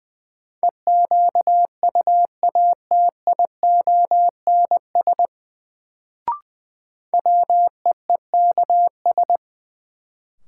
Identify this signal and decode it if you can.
Morse code, heard as a keyed tone, about 20 wpm, 700 Hz. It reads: EQUATIONS WEEKS
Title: Sets of 5 Words - New Word Drill - 20wpm